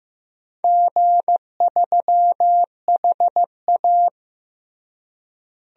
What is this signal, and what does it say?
Morse code, heard as a keyed tone, about 15 wpm, 700 Hz. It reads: G3HA
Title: DXCC-100 Call Signs - 4 Characters - Repeated 3x - 15wpm